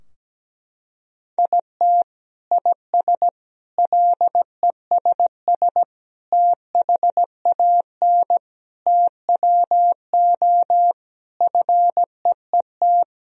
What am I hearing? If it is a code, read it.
Morse code, heard as a keyed tone, about 17 wpm, 700 Hz. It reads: IT IS LESS THAN TWO FEET
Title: Sentences from Top 1000 Words - Review Part 2 - 17wpm